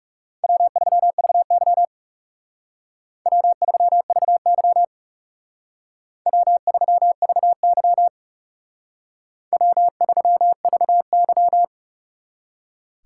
Morse code, heard as a keyed tone, about 34 wpm, 700 Hz. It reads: W3VY W3VY W3VY W3VY
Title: Speed Racing - USA Call Signs 30wpm